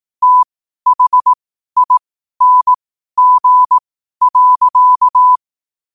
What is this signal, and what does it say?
Morse code, heard as a keyed tone, about 18 wpm, spaced as for 12 wpm, 1000 Hz. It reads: THING.
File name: qotdfiles / 12wpm